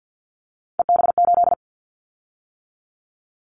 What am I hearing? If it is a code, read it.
Morse code, heard as a keyed tone, about 50 wpm, 700 Hz. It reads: EB8
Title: Callsign Prefixes for Top 50 DXCC - 50wpm